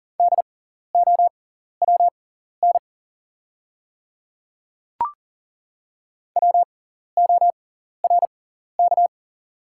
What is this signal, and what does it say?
Morse code, heard as a keyed tone, about 40 wpm, spaced as for 14 wpm, 700 Hz. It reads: DOWN WORK
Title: 6x Character Spacing - Top 200 Words - 40wpm